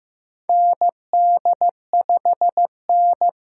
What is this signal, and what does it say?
Morse code, heard as a keyed tone, about 15 wpm, 700 Hz. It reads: ND5N
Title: USA Call Signs - 4 Characters - 8x Character Spacing - 15wpm